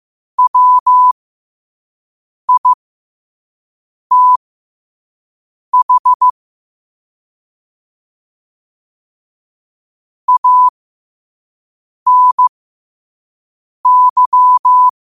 Morse code, heard as a keyed tone, about 15 wpm, spaced as for 5 wpm, 1000 Hz. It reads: WITH ANY
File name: qotdfiles / 5wpm